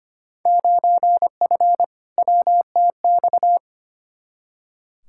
Morse code, heard as a keyed tone, about 25 wpm, 700 Hz. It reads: 9F WTX
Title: Field Day Exchanges Encore - 25wpm